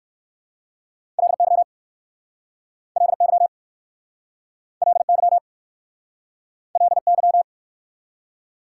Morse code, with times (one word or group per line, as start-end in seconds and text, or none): 1.2-1.6 LY
3.0-3.5 LY
4.8-5.4 LY
6.8-7.4 LY